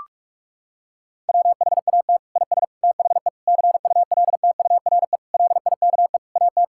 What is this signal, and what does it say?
Morse code, heard as a keyed tone, about 46 wpm, 700 Hz. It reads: WHAT IS THE CULTURE LIKE AT
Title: Sentences from Top 1000 Words - Review Part 6 - 45wpm